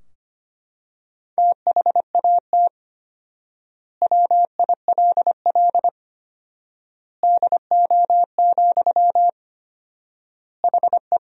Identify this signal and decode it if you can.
Morse code, heard as a keyed tone, about 25 wpm, 700 Hz. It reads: THAT WILL DO, HE
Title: 4x Word-Spacing - Sentences from Top 100 Words - 25wpm